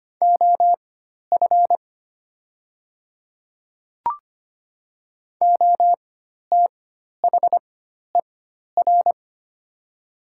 Morse code, heard as a keyed tone, about 25 wpm, spaced as for 12 wpm, 700 Hz. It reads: OF OTHER
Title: 4x Character Spacing - Top 100 Words - 25wpm